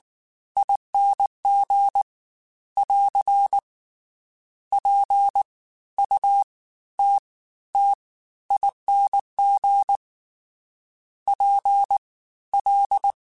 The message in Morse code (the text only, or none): ING <AR> PUTTING PL